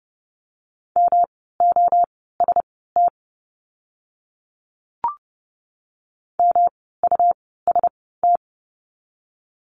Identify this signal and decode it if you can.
Morse code, heard as a keyed tone, about 30 wpm, spaced as for 17 wpm, 700 Hz. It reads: MOST MUST
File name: course / 30:10wpm / 237 Lesson 38f - Letter q - Words 30wpm